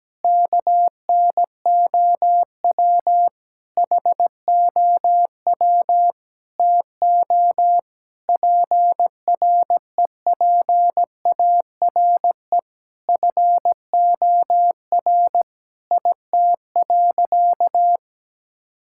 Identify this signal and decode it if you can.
Morse code, heard as a keyed tone, about 17 wpm, 700 Hz. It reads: KNOW HOW TO PREPARE FOR IT.
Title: Sentences from Top 1000 Words - Review Part 6 - 17wpm